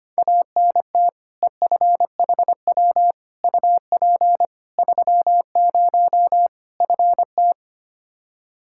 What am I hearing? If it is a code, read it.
Morse code, heard as a keyed tone, about 25 wpm, 700 Hz. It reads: ANT EFHW UP 30 FT
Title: Practice QSO - Exchange 2 - Info - 25wpm